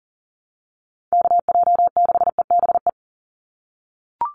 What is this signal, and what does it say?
Morse code, heard as a keyed tone, about 40 wpm, 700 Hz. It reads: KJ6EBE
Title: USA Call Signs 40wpm